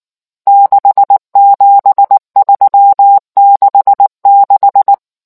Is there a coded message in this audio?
Yes: Morse code from 0.5 to 4.9 s, heard as a keyed tone, about 19 wpm, 800 Hz. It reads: 67366